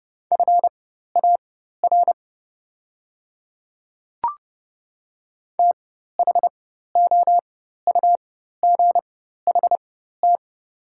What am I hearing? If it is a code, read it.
Morse code, heard as a keyed tone, about 30 wpm, spaced as for 14 wpm, 700 Hz. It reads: FAR THOUGHT
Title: 4x Character Spacing - Top 500 Words - 30wpm